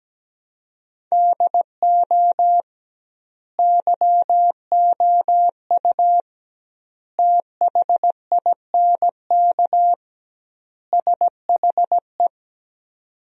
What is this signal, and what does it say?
Morse code, heard as a keyed tone, about 17 wpm, 700 Hz. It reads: DO YOU THINK SHE
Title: 2x Word-Spacing - Sentences from Top 200 Words - 17wpm